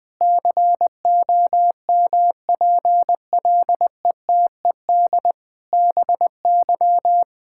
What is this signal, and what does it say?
Morse code, heard as a keyed tone, about 20 wpm, 700 Hz. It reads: COMPLETED BY
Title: Sentences with 8-Word Phrases - 20wpm